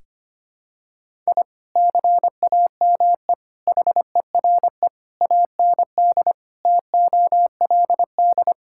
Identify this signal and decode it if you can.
Morse code, heard as a keyed tone, about 25 wpm, 700 Hz. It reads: I CAME HERE AND TOLD
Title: Sentences from Top 1000 Words - Review Part 6 - 25wpm